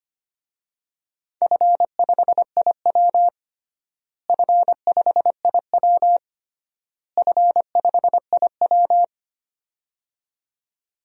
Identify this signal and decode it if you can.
Morse code, heard as a keyed tone, about 25 wpm, 700 Hz. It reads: F5IW F5IW F5IW
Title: DXCC-100 Call Signs - 4 Characters - Repeated 3x - 25wpm